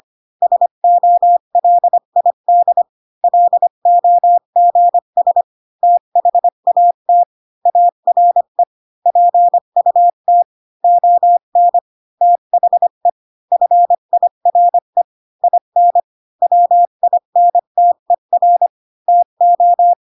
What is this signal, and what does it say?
Morse code, heard as a keyed tone, about 25 wpm, 700 Hz. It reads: SOLID LOGS THAT ARE PUT ON THE FIRE IN WINTER TO
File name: Pinocchio / Pinocchio ch1 20 25 ik1hge